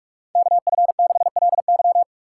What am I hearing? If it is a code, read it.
Morse code, heard as a keyed tone, about 46 wpm, 700 Hz. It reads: KF6LY
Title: USA Call Signs - 5 Characters - 3x Character Spacing - 45wpm